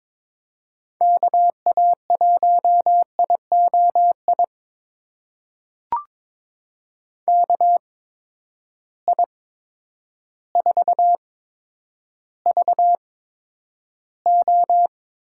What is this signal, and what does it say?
Morse code, heard as a keyed tone, about 22 wpm, spaced as for 10 wpm, 700 Hz. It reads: KA1IOI KI4VO
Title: USA Call Signs - 6 Characters - 8x Character Spacing - 22wpm